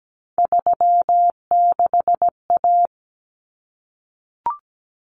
Morse code, heard as a keyed tone, about 17 wpm, 700 Hz. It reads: 36A
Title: Common Cut Numbers - 3-Digit 17wpm